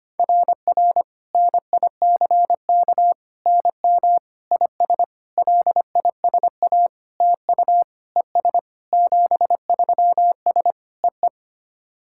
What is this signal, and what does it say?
Morse code, heard as a keyed tone, about 25 wpm, 700 Hz. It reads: RR NICK NM IS LISA TU ES 73S EE
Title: SOTA QSO Practice 25wpm